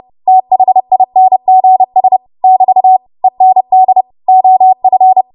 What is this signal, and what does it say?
Morse code, heard as a keyed tone, about 30 wpm, 750 Hz. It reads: THINGS <BT> END OF
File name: Morse / 30WPM